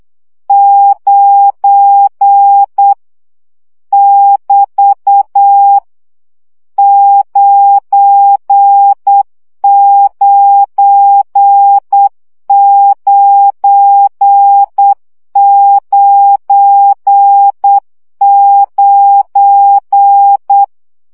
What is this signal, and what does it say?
Morse code, heard as a keyed tone, about 8 wpm, 800 Hz. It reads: 9 <BT> 99999
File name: CW gyakorlas 9 7 baud